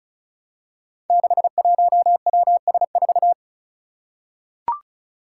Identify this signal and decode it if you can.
Morse code, heard as a keyed tone, about 34 wpm, 700 Hz. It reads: 61WS4
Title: Five Challenging Letter-Number Combinations 35wpm